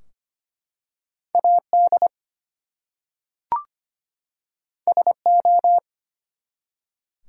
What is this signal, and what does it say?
Morse code, heard as a keyed tone, about 25 wpm, 700 Hz. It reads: AD SO
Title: Two Letter Words - 25wpm